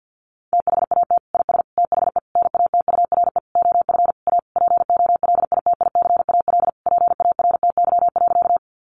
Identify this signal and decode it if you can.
Morse code, heard as a keyed tone, about 50 wpm, 700 Hz. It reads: THAT IS THE NATURE OF A POLITICAL PARTY.
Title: Sentences from Top 1400 Words 50wpm